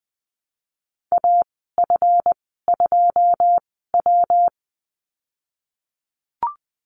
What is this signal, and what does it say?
Morse code, heard as a keyed tone, about 20 wpm, spaced as for 15 wpm, 700 Hz. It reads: AF2W
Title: 179 Lesson 29g - Slash - Call Signs 20wpm